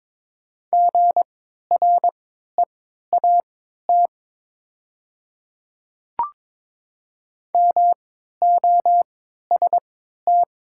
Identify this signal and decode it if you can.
Morse code, heard as a keyed tone, about 22 wpm, spaced as for 12 wpm, 700 Hz. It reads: GREAT MOST
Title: 3x Character Spacing - Top 200 Words - 22wpm